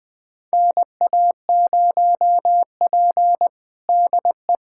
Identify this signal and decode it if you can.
Morse code, heard as a keyed tone, about 20 wpm, 700 Hz. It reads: NA0P DE
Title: USA Callsign DE Callsign - 4 Characters - 20wpm